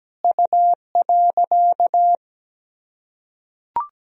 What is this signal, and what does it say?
Morse code, heard as a keyed tone, about 17 wpm, 700 Hz. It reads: U.
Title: Sentences from Top 300 Words and Abbreviations 17wpm